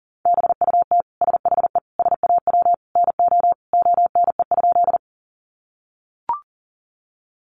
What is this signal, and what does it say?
Morse code, heard as a keyed tone, about 40 wpm, 700 Hz. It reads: BUT SHE SAW NO ONE?
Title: Sentences from Top 300 Words 40wpm